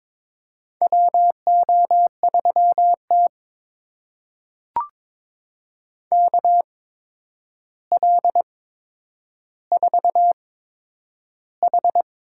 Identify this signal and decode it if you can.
Morse code, heard as a keyed tone, about 22 wpm, spaced as for 9 wpm, 700 Hz. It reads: WO3T KL4H
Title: USA Call Signs - 4 Characters - 8x Character Spacing - 22wpm